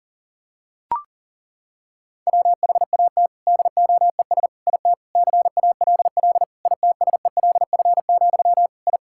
Morse code, heard as a keyed tone, about 40 wpm, 700 Hz. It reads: WHAT DOES IT CALL ITSELF, I
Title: Sentences from Top 1000 Words - Review Part 5 - 40wpm